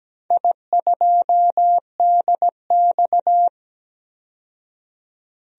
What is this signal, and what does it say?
Morse code, heard as a keyed tone, about 17 wpm, 700 Hz. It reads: I2DX
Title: DXCC-100 Call Signs - 4 Characters - 17wpm